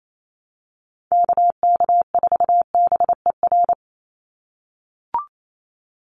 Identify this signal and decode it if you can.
Morse code, heard as a keyed tone, about 28 wpm, 700 Hz. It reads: KK4BER
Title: USA Call Signs 28wpm